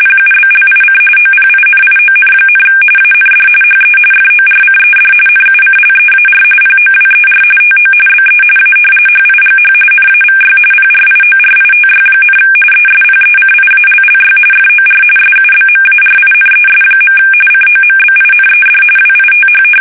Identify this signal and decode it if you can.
RTTY, about 45.45 baud. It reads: PUBLIC OPINION, AND HE FORCED NUCLEAR POWER TO BE A HIGH NAVY PRIORITY, ESPECIALLY FOR SUBMARINES. COMBINED WITH
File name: fsk850 / usnavy5